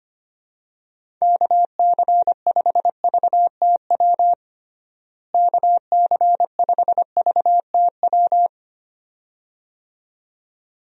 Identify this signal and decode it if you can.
Morse code, heard as a keyed tone, about 25 wpm, 700 Hz. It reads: KC5VTW KC5VTW
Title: USA Call Signs - 6 Characters - Repeated 2x - 25wpm